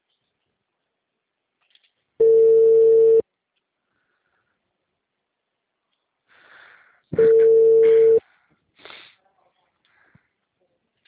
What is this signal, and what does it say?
Ringback tone (China, 450 Hz).